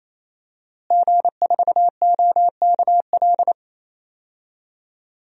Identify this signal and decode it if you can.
Morse code, heard as a keyed tone, about 28 wpm, 700 Hz. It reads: G4OKL
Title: DXCC-100 Call Signs - 5 Characters - Repeated 3x - 28wpm